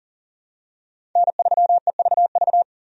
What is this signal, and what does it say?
Morse code, heard as a keyed tone, about 40 wpm, 700 Hz. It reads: N3EVV
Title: USA Call Signs - 5 Characters - 40wpm